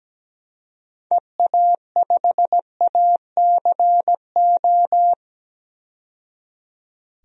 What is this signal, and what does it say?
Morse code, heard as a keyed tone, about 17 wpm, 700 Hz. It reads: EA5ACO
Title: DXCC-100 Call Signs - 6 Characters - 17wpm